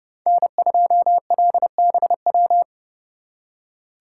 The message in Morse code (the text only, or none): N2LBW